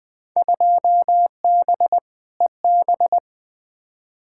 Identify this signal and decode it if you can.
Morse code, heard as a keyed tone, about 21 wpm, 700 Hz. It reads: 2B EB
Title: Field Day Exchanges - 20wpm